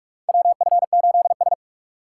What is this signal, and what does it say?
Morse code, heard as a keyed tone, about 46 wpm, 700 Hz. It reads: WF8S
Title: USA Call Signs - 4 Characters - 4x Character Spacing - 45wpm